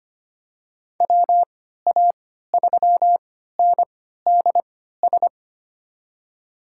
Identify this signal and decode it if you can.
Morse code, heard as a keyed tone, about 25 wpm, spaced as for 14 wpm, 700 Hz. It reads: WA3NDS
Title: USA Call Signs - 6 Characters - 3x Character Spacing - 25wpm